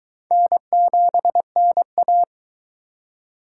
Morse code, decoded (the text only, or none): N7NA